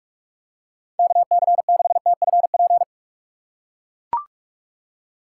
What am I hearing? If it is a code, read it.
Morse code, heard as a keyed tone, about 43 wpm, 700 Hz. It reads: KC6TFP
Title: USA Call Signs - 3x Character Spacing - 45wpm